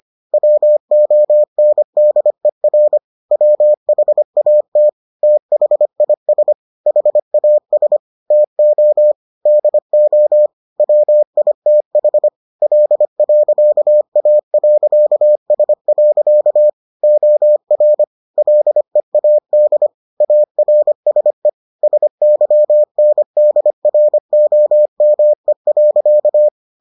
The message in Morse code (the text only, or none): WONDER WHAT THIS HAS TO DO WITH L.A.S. OR LEAD ARSE SYNDROME.